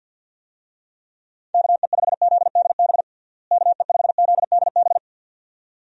Morse code, heard as a keyed tone, about 51 wpm, 700 Hz. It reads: KE5ZDB KE5ZDB
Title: USA Call Signs - 6 Characters - Repeated 2x - 50wpm